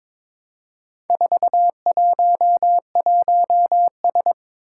Morse code, heard as a keyed tone, about 23 wpm, 700 Hz. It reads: 411S
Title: Four Challenging Letter-Number Combinations 22wpm